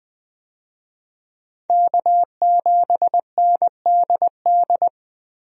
Morse code, heard as a keyed tone, about 21 wpm, 700 Hz. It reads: K7NDD
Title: USA Call Signs - 5 Characters - Repeated 3x - 20wpm